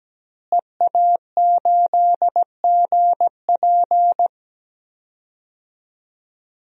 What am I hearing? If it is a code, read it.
Morse code, heard as a keyed tone, about 17 wpm, 700 Hz. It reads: EA8GP
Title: DXCC-100 Call Signs - 5 Characters - Repeated 2x - 17wpm